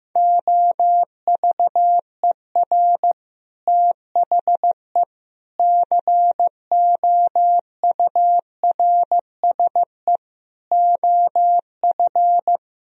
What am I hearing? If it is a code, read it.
Morse code, heard as a keyed tone, about 16 wpm, 700 Hz. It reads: OVER THE COURSE OF
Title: Speed-Racing - Sets of 4 Words 15wpm